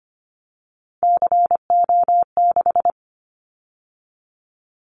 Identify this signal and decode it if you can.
Morse code, heard as a keyed tone, about 25 wpm, 700 Hz. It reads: CO6
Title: Callsign Prefixes for Top 100 DXCC - 25wpm